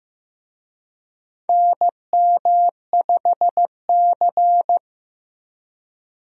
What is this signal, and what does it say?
Morse code, heard as a keyed tone, about 15 wpm, 700 Hz. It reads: NM5C
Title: USA Call Signs - 4 Characters - 15wpm